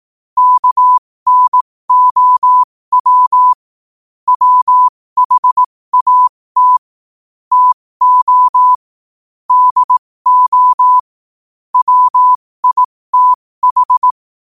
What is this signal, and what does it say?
Morse code, heard as a keyed tone, about 18 wpm, spaced as for 15 wpm, 1000 Hz. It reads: KNOW WHAT TO DO WITH
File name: qotdfiles / 15wpm